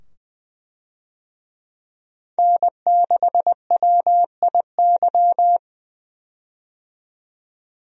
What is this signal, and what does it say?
Morse code, heard as a keyed tone, about 20 wpm, 700 Hz. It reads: N6WIY
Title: USA Call Signs - 5 Characters - 20wpm